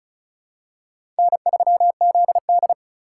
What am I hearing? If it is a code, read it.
Morse code, heard as a keyed tone, about 36 wpm, 700 Hz. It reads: N3ZD
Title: USA Call Signs - 3x Character Spacing - 35wpm